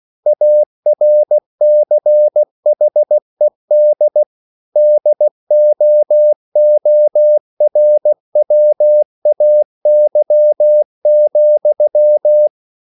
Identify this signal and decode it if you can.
Morse code, heard as a keyed tone, about 16 wpm, 600 Hz. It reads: ARCHED DOORWAY,